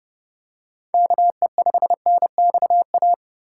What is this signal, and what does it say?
Morse code, heard as a keyed tone, about 31 wpm, 700 Hz. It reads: KE5NXA
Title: USA Call Signs - 6 Characters - 4x Character Spacing - 30wpm